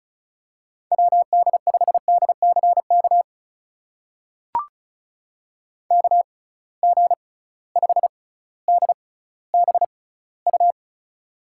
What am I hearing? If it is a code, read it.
Morse code, heard as a keyed tone, about 36 wpm, spaced as for 18 wpm, 700 Hz. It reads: WD5DCK KG5DBU
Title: USA Call Signs - 6 Characters - 6x Character Spacing - 35wpm